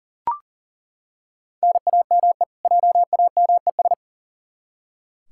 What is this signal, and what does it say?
Morse code, heard as a keyed tone, about 40 wpm, 700 Hz. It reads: NAME JAMES
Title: Practice QSO - Exchange 1 - Beginning - 40wpm